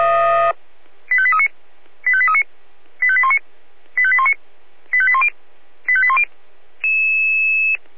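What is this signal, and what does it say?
Five-tone selective call (ZVEI), heard as quick runs of five tones, tones of about 70 ms. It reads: 85839 85839 85819 85819 85810 85810
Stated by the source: ZVEI five-tone selective call received on SDR.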